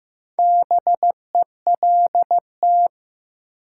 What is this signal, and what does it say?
Morse code, heard as a keyed tone, about 15 wpm, 700 Hz. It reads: BELT
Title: Sentences from Top 1400 Words - New Word Drill Alt - 15wpm